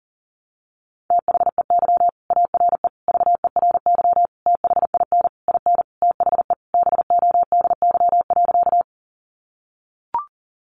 Morse code, heard as a keyed tone, about 40 wpm, 700 Hz. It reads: THEY ARE VERY THIN IN THE BODY.
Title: Sentences from Top 1200 Words 40wpm